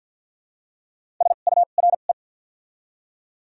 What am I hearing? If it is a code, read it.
Morse code, heard as a keyed tone, about 53 wpm, spaced as for 37 wpm, 700 Hz. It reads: SURE
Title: 2x Character Spacing - Top 300 Words - 50wpm